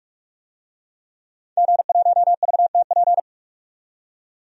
Morse code, heard as a keyed tone, about 45 wpm, 700 Hz. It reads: G1VTP
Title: DXCC-100 Call Signs - 5 Characters - 45wpm